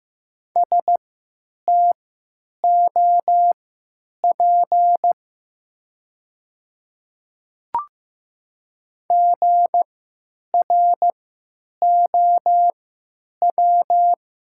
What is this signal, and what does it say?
Morse code, heard as a keyed tone, about 15 wpm, spaced as for 9 wpm, 700 Hz. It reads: STOP GROW
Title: 3x Character Spacing - Top 300 Words - 15wpm